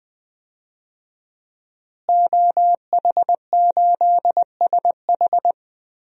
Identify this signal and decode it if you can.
Morse code, heard as a keyed tone, about 20 wpm, 700 Hz. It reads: OH8SH
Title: DXCC-100 Call Signs - 5 Characters - Repeated 3x - 20wpm